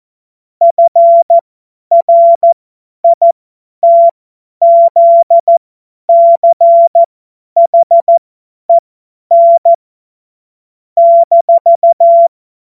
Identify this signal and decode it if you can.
Morse code, heard as a keyed tone, about 14 wpm, spaced as for 10 wpm, 680 Hz. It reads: FRITZCHEN -
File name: Morsekurs / Witze / Witz 14. März 2023 50 F70